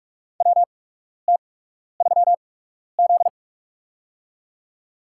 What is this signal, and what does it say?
Morse code, heard as a keyed tone, about 48 wpm, spaced as for 12 wpm, 700 Hz. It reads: WT3Z
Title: USA Call Signs - 4 Characters - 8x Character Spacing - 45wpm